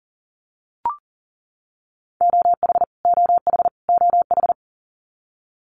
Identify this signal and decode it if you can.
Morse code, heard as a keyed tone, about 40 wpm, 700 Hz. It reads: OH OH OH
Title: US States Repeated 3x 40wpm